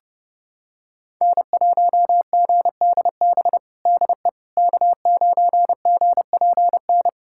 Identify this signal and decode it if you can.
Morse code, heard as a keyed tone, about 29 wpm, 700 Hz. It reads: N1GDB DE K9GPN
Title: USA Callsign DE Callsign - 5 Characters - 30wpm